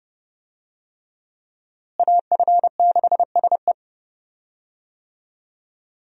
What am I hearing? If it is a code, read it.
Morse code, heard as a keyed tone, about 32 wpm, 700 Hz. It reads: AF6SE
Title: USA Call Signs - 5 Characters - Repeated 3x - 30wpm